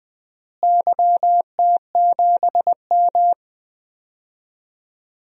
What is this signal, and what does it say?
Morse code, heard as a keyed tone, about 20 wpm, 700 Hz. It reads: YT7M
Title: DXCC-100 Call Signs - 4 Characters - Repeated 3x - 20wpm